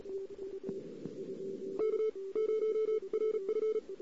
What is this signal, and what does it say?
Morse code, heard as a keyed tone, about 38 wpm, spaced as for 28 wpm, 2200 Hz, with noise. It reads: K0RF
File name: CW / 2008-ss-cw-k5tr-op-n5rz -14:00